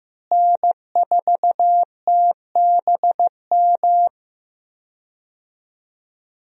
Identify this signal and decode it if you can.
Morse code, heard as a keyed tone, about 15 wpm, 700 Hz. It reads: N4TBM
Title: USA Call Signs - 5 Characters - Repeated 3x - 15wpm